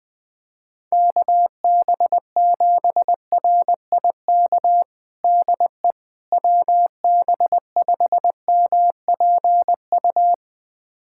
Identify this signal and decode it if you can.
Morse code, heard as a keyed tone, about 21 wpm, 700 Hz. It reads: KB7RIK DE WB5MPU
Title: USA Callsign DE Callsign - 6 Characters - 20wpm